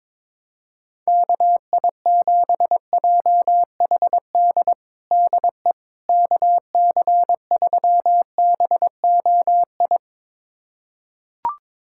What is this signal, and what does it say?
Morse code, heard as a keyed tone, about 22 wpm, 700 Hz. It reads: KI7JHD DE KC3BOI
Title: USA Callsign DE Callsign 22wpm